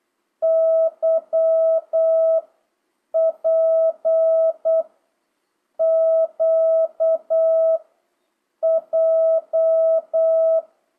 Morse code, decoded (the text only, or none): Y P Q J